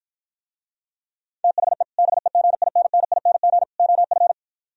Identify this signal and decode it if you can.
Morse code, heard as a keyed tone, about 52 wpm, 700 Hz. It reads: THE BEGINNING OF